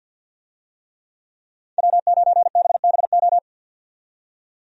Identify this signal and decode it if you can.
Morse code, heard as a keyed tone, about 50 wpm, 700 Hz. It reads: W9BBO
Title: USA Call Signs - 5 Characters - 50wpm